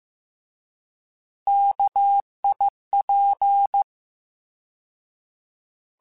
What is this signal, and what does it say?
Morse code, heard as a keyed tone, about 15 wpm, 800 Hz. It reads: KIP
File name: Alphabetical / Alphabetical mp3s / Ks @13